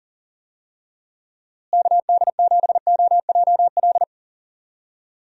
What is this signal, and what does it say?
Morse code, heard as a keyed tone, about 40 wpm, 700 Hz. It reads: KD7OJL
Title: USA Call Signs - 6 Characters - Repeated 2x - 40wpm